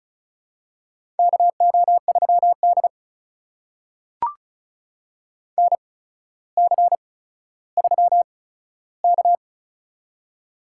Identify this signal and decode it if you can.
Morse code, heard as a keyed tone, about 36 wpm, spaced as for 15 wpm, 700 Hz. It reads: KO3D NC3K
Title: USA Call Signs - 4 Characters - 8x Character Spacing - 35wpm